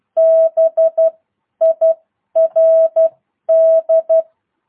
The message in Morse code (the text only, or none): BIRD